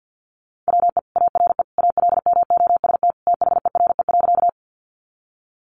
Morse code, heard as a keyed tone, about 50 wpm, 700 Hz. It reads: WE ARE ALMOST THERE.
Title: Sentences from Top 700 Words 50wpm